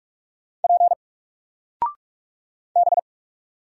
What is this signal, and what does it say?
Morse code, heard as a keyed tone, about 44 wpm, 700 Hz. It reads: P B
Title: Single Letter-Number 45wpm